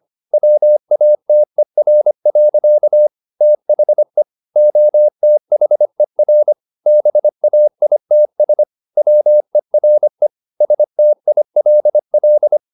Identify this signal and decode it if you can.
Morse code, heard as a keyed tone, about 25 wpm, 600 Hz. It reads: WATER. THE OTHER BAITS WERE STILL
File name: the old man and the sea part 1 25wpm